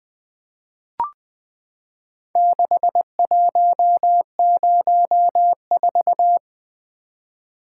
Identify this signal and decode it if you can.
Morse code, heard as a keyed tone, about 20 wpm, 700 Hz. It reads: 6104
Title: Four Letter-Number Combinations - 20wpm